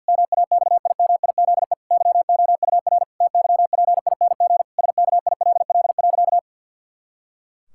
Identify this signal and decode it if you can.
Morse code, heard as a keyed tone, about 50 wpm, 700 Hz. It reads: MAXIMIZE YOUR TYPING SKILL.
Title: Pangram Challenge v2 - 50wpm